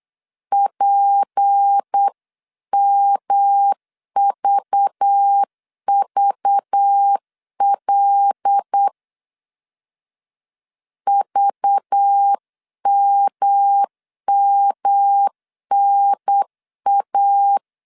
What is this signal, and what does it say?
Morse code, heard as a keyed tone, about 8 wpm, 800 Hz. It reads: PMVVL VMMNA
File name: 8wpm